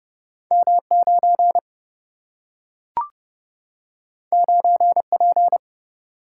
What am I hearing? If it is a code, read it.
Morse code, heard as a keyed tone, about 30 wpm, 700 Hz. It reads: M 9 9 P
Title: Single-Letter Single-Number 30wpm